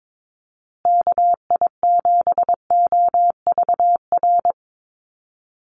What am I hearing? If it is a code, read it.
Morse code, heard as a keyed tone, about 22 wpm, 700 Hz. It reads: KI7OVR
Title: USA Call Signs Repeated 3x 22wpm